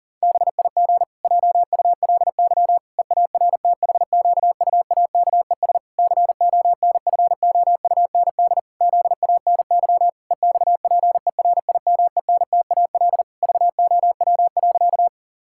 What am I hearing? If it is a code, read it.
Morse code, heard as a keyed tone, about 40 wpm, 700 Hz. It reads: BIG JULY EARTHQUAKES CONFOUND ZANY EXPERIMENTAL VOW.
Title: Pangram Challenge v2 - 40wpm